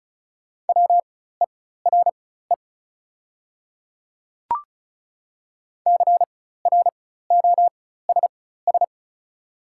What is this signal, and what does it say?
Morse code, heard as a keyed tone, about 36 wpm, spaced as for 16 wpm, 700 Hz. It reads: WERE CROSS
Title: 4x Character Spacing - Top 300 Words - 35wpm